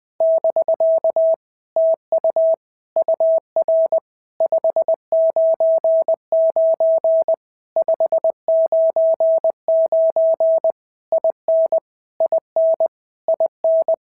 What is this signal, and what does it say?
Morse code, heard as a keyed tone, about 20 wpm, 650 Hz. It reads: <BK> TU UR 599 599 IN IN IN
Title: POTA QSO Practice - 20wpm